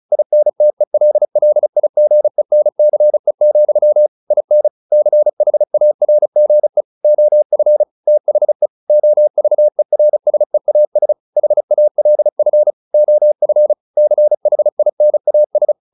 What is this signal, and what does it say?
Morse code, heard as a keyed tone, about 34 wpm, 600 Hz. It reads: INTELLIGENCE, IN CHARGE OF THE OVERSEAS HALF OF CHINAS